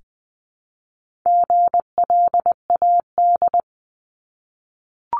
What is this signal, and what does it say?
Morse code, heard as a keyed tone, about 20 wpm, 700 Hz. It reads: GLAD
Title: Multi-letter Phonemes 20wpm